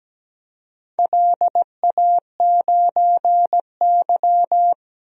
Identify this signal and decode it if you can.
Morse code, heard as a keyed tone, about 17 wpm, 700 Hz. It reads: LA9Y
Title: DXCC-100 Call Signs - 4 Characters - Repeated 3x - 17wpm